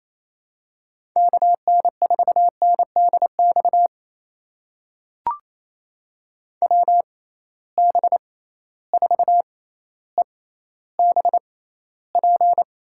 Morse code, heard as a keyed tone, about 27 wpm, spaced as for 14 wpm, 700 Hz. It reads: KN4NDX WB4EBP
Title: USA Call Signs - 6 Characters - 6x Character Spacing - 28wpm